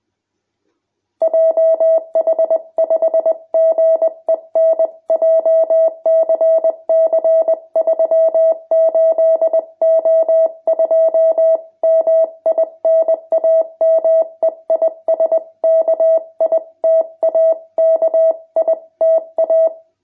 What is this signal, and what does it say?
Morse code, heard as a keyed tone, about 20 wpm, 650 Hz. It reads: JH5GEN JCC 38O2 MI NAME IS KITA KITA
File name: jh8cbh / 2021cw